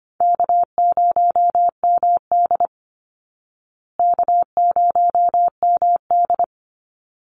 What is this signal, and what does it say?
Morse code, heard as a keyed tone, about 25 wpm, 700 Hz. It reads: K0MD K0MD
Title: Easy Call Signs Repeated 2x - 25wpm